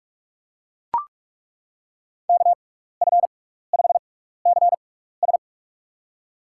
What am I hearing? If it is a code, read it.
Morse code, heard as a keyed tone, about 42 wpm, spaced as for 15 wpm, 700 Hz. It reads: KF5CS
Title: USA Call Signs - 5 Characters - 6x Character Spacing - 45wpm